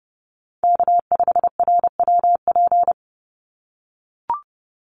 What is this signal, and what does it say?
Morse code, heard as a keyed tone, about 30 wpm, 700 Hz. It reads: K5RWP
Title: NA SOTA Chaser Callsigns 30wpm